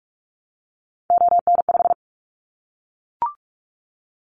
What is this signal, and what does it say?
Morse code, heard as a keyed tone, about 44 wpm, 700 Hz. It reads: ON5
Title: Callsign Prefixes for Top 25 DXCC - Repeated 3x - 45wpm